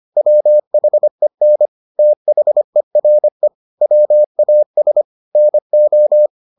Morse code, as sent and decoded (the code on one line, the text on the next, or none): .-- .... -. / - .... .-. / .-- .- ... / -. ---
WHEN THERE WAS NO